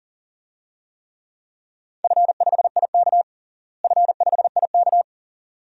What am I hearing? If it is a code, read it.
Morse code, heard as a keyed tone, about 40 wpm, 700 Hz. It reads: F5IK F5IK
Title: DXCC-100 Call Signs - 4 Characters - Repeated 2x - 40wpm